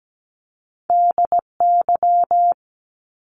Morse code, heard as a keyed tone, about 17 wpm, 700 Hz. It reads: DY
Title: Top Two Letter Combinations in Words 17wpm